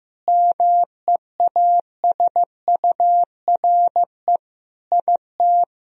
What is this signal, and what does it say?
Morse code, heard as a keyed tone, about 15 wpm, 700 Hz. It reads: MEASURE IT
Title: Sentences from Top 1000 Words - Review Part 2 - 15wpm